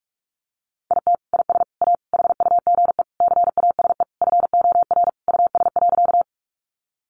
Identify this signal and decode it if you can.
Morse code, heard as a keyed tone, about 44 wpm, 700 Hz. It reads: IT IS A HUGE CASE FOR US.
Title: Sentences from Top 1100 Words 45wpm